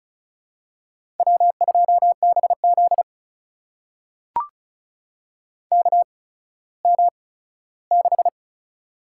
Morse code, heard as a keyed tone, about 36 wpm, spaced as for 17 wpm, 700 Hz. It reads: W2BZ KM6
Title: USA Call Signs - 4 Characters - 8x Character Spacing - 35wpm